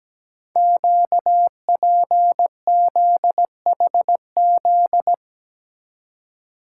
Morse code, heard as a keyed tone, about 17 wpm, 700 Hz. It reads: QPZHZ
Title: Code Groups - 5 Letters - 17wpm